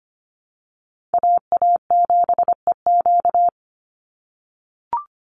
Morse code, heard as a keyed tone, about 25 wpm, 700 Hz. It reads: AA7EQ
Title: NA SOTA Chaser Callsigns 25wpm